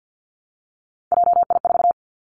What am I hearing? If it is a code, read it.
Morse code, heard as a keyed tone, about 50 wpm, 700 Hz. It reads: JI4
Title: Three Challenging Letter-Number Combinations 50wpm